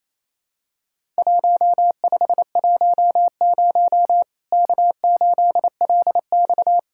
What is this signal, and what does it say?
Morse code, heard as a keyed tone, about 28 wpm, 700 Hz. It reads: 1510 K8LX
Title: Worked All Europe DX Contest - QTC Exchange - 28wpm